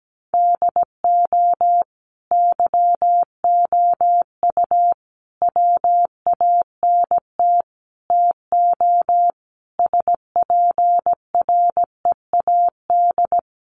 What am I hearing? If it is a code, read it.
Morse code, heard as a keyed tone, about 17 wpm, 700 Hz. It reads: DO YOU WANT TO SPREAD
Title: Sentences from Top 1200 Words 17wpm